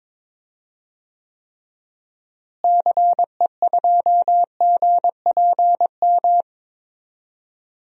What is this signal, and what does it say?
Morse code, heard as a keyed tone, about 22 wpm, 700 Hz. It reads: CE2GPM
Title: DXCC-100 Call Signs - 6 Characters - Repeated 3x - 22wpm